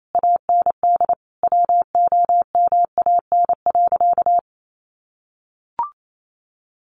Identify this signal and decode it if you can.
Morse code, heard as a keyed tone, about 28 wpm, 700 Hz. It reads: AND WOMAN.
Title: Sentences from Top 1100 Words 28wpm